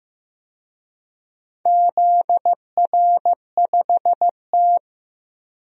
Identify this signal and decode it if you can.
Morse code, heard as a keyed tone, about 15 wpm, 700 Hz. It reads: ZR5T
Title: DXCC-100 Call Signs - 4 Characters - Repeated 2x - 15wpm